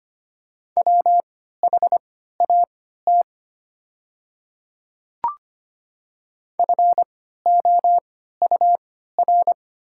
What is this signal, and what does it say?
Morse code, heard as a keyed tone, about 25 wpm, spaced as for 14 wpm, 690 Hz. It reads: WHAT FOUR
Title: 3x Character Spacing - Top 300 Words - 25wpm